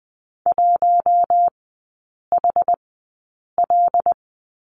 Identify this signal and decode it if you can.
Morse code, heard as a keyed tone, about 19 wpm, spaced as for 8 wpm, 700 Hz. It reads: 1HL
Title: 074 Lesson 13e - Letter L - Three Characters 20wpm